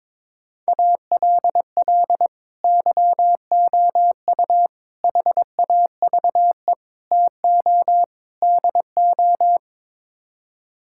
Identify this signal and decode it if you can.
Morse code, heard as a keyed tone, about 22 wpm, 700 Hz. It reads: ALL YOU HAVE TO DO
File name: practice / 22wpm / Build A Phrase - 5-Word Phrases - 22wpm